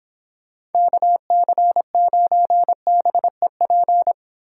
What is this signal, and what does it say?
Morse code, heard as a keyed tone, about 31 wpm, 700 Hz. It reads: KC9BEP